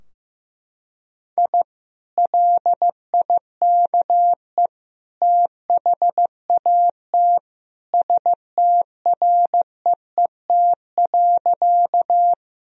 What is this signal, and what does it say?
Morse code, heard as a keyed tone, about 16 wpm, 700 Hz. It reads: I LIKE THAT STREET.
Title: Sentences from Top 1000 Words - Review Part 3 - 15wpm